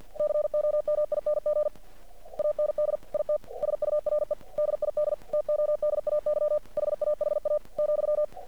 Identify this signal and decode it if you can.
Morse code, heard as a keyed tone, about 50 wpm, 600 Hz. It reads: COMING AND IT SURE DID TODAY HAHA ,
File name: archived / CRSnet